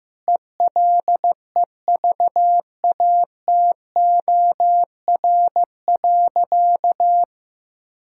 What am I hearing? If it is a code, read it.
Morse code, heard as a keyed tone, about 15 wpm, 700 Hz. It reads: ELEVATOR.
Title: Sentences from Top 1600 Words 15wpm